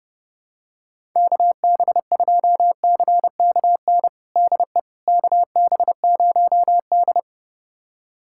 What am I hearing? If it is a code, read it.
Morse code, heard as a keyed tone, about 32 wpm, 700 Hz. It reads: KB2CKN DE KB0D
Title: USA Callsign DE Callsign 30wpm